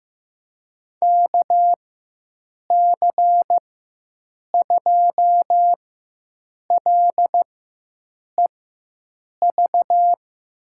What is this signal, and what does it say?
Morse code, heard as a keyed tone, about 15 wpm, spaced as for 7 wpm, 700 Hz. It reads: KC2LEV
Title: USA Call Signs - 4x Character Spacing - 15wpm